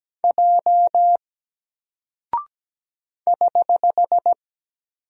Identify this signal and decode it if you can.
Morse code, heard as a keyed tone, about 17 wpm, 700 Hz. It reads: J <HH>
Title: Single Character (Letter:Number:Punctuation:Prosign) Alt 17wpm